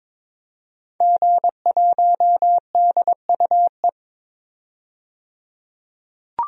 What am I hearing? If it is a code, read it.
Morse code, heard as a keyed tone, about 22 wpm, 700 Hz. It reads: G1DUE
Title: DXCC-100 Call Signs - 5 Characters - 22wpm